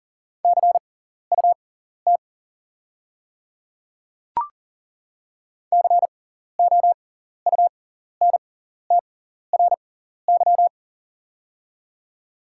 Morse code, heard as a keyed tone, about 38 wpm, spaced as for 14 wpm, 690 Hz. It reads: CUT COUNTRY
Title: 6x Character Spacing - Top 300 Words - 40wpm